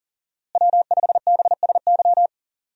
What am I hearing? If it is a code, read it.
Morse code, heard as a keyed tone, about 40 wpm, 700 Hz. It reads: W5BSY
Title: USA Call Signs - 5 Characters - Repeated 2x - 40wpm